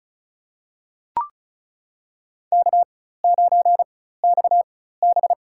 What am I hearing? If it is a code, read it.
Morse code, heard as a keyed tone, about 36 wpm, spaced as for 16 wpm, 700 Hz. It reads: K9XB
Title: USA Call Signs - 4 Characters - 4x Character Spacing - 35wpm